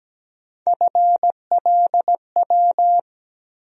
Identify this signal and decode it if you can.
Morse code, heard as a keyed tone, about 17 wpm, 700 Hz. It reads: FLW
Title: USA Call Sign Postfixes - 3 Letters 17wpm